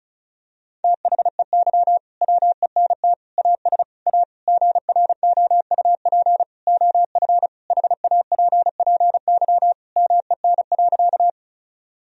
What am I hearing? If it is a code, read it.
Morse code, heard as a keyed tone, about 34 wpm, 700 Hz. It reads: THEY WENT AS A GROUP OF HAPPY MEN.
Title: Sentences from Top 1000 Words - Review Part 1 - 35wpm